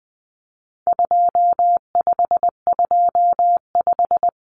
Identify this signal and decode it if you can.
Morse code, heard as a keyed tone, about 20 wpm, 700 Hz. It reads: 2525
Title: Common Cut Numbers - 4-Digit 20wpm